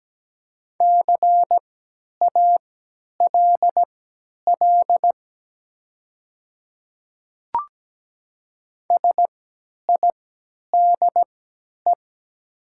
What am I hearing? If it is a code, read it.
Morse code, heard as a keyed tone, about 17 wpm, spaced as for 10 wpm, 700 Hz. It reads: CALL SIDE
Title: 3x Character Spacing - Top 100 Words - 17wpm